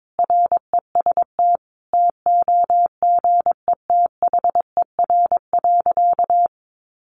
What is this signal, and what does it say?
Morse code, heard as a keyed tone, about 22 wpm, 700 Hz. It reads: REST TOGETHER.
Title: Sentences from Top 500 Words 22wpm